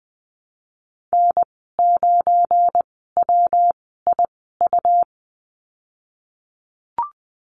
Morse code, heard as a keyed tone, about 20 wpm, spaced as for 15 wpm, 700 Hz. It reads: N9WIU
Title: 193 Lesson 31g - Letter v - Call Signs 20wpm